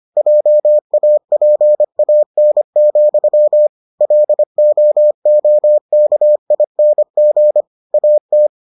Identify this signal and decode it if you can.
Morse code, heard as a keyed tone, about 25 wpm, 600 Hz. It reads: JAPAN, LOOKING AT